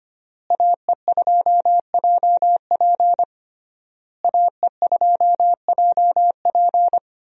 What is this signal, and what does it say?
Morse code, heard as a keyed tone, about 25 wpm, 700 Hz. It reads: AE2JP AE2JP
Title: USA Call Signs - 5 Characters - Repeated 3x - 25wpm